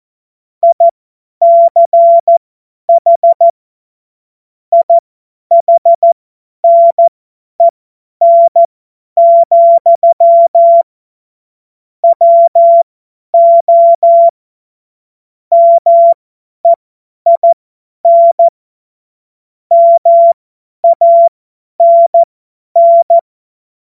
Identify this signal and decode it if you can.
Morse code, heard as a keyed tone, about 14 wpm, spaced as for 10 wpm, 680 Hz. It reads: ICH IHNEN, WO MEIN MANN
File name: Morsekurs / Witze / Witz 23 Januar 2024 50 F70